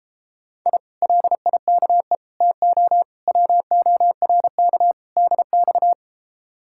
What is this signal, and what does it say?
Morse code, heard as a keyed tone, about 33 wpm, 700 Hz. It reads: I LIKE TO WORK DX